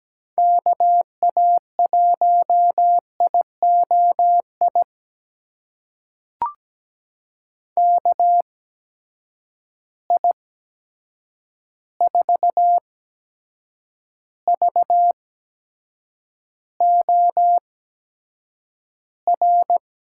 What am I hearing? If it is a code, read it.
Morse code, heard as a keyed tone, about 17 wpm, spaced as for 7 wpm, 700 Hz. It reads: KA1IOI KI4VOR
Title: USA Call Signs - 6 Characters - 8x Character Spacing - 17wpm